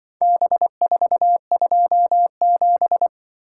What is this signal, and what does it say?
Morse code, heard as a keyed tone, about 25 wpm, 700 Hz. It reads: B427
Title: Speed-Racing - Four Letter-Number Combinations - 20wpm